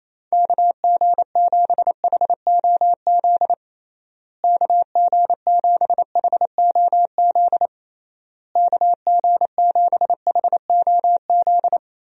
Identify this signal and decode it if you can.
Morse code, heard as a keyed tone, about 28 wpm, 700 Hz. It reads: KG7HOZ KG7HOZ KG7HOZ
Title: USA Call Signs - 6 Characters - Repeated 3x - 28wpm